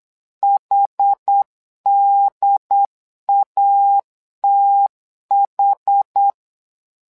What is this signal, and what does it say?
Morse code, heard as a keyed tone, about 8 wpm, 800 Hz. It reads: HDATH
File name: LEZIONI / 2 UHDN / 2 UHD 8wpm 2258